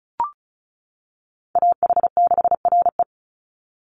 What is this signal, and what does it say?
Morse code, heard as a keyed tone, about 34 wpm, 700 Hz. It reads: AH6RE
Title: Call Signs - 35wpm